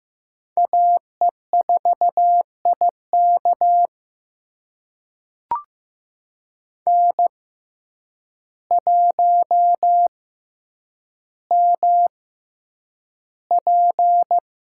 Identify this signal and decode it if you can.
Morse code, heard as a keyed tone, about 15 wpm, spaced as for 8 wpm, 700 Hz. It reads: AE4IK N1MP
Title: USA Call Signs - 5 Characters - 6x Character Spacing - 15wpm